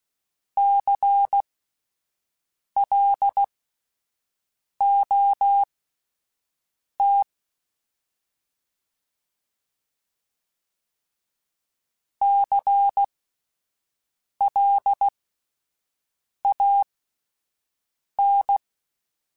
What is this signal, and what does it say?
Morse code, heard as a keyed tone, about 16 wpm, spaced as for 5 wpm, 800 Hz. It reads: CLOT CLAN
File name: Spaced Alphas / Spaced Alphas mp3s / Cs sp @10